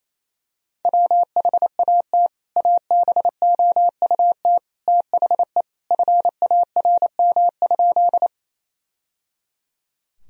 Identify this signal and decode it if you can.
Morse code, heard as a keyed tone, about 28 wpm, 700 Hz. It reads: WHAT ABOUT THE FARM?
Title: Sentences from Top 1000 Words - Review Part 2 - 28wpm